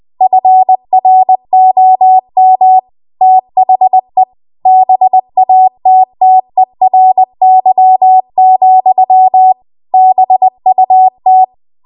Morse code, heard as a keyed tone, about 20 wpm, 750 Hz. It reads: FROM THE BATTERY, BUT
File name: Archive / 20 WPM / 20WPM